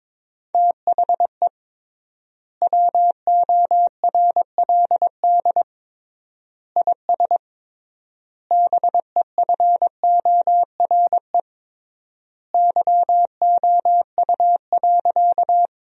Morse code, heard as a keyed tone, about 21 wpm, 700 Hz. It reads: THE WORLD IS BEFORE YOU.
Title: 3x Word-Spacing - Sentences from Top 200 Words - 22wpm